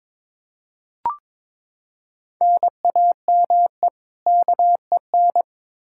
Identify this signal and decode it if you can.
Morse code, heard as a keyed tone, about 22 wpm, 700 Hz. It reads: NAME KEN
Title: Practice QSO - Exchange 1 - Beginning - 22wpm